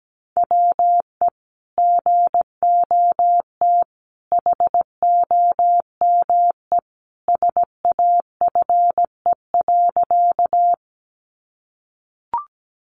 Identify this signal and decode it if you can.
Morse code, heard as a keyed tone, about 17 wpm, 700 Hz. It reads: WE GOT HOME SAFE.
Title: Sentences from Top 900 Words 17wpm